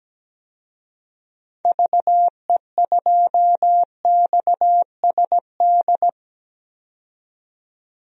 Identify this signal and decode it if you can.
Morse code, heard as a keyed tone, about 17 wpm, 700 Hz. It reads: VE2XSD
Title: DXCC-100 Call Signs - 6 Characters - 17wpm